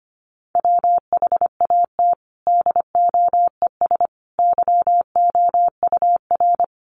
Morse code, heard as a keyed tone, about 25 wpm, 700 Hz. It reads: WHAT DOES YOUR
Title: Sentences from Top 1500 Words 25wpm